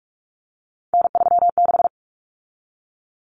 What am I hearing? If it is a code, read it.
Morse code, heard as a keyed tone, about 44 wpm, 700 Hz. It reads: N36
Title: Common Cut Numbers - 3-Digit 45wpm